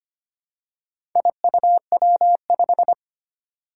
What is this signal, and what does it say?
Morse code, heard as a keyed tone, about 25 wpm, 700 Hz. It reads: IUW5
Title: Four Challenging Letter-Number Combinations 25wpm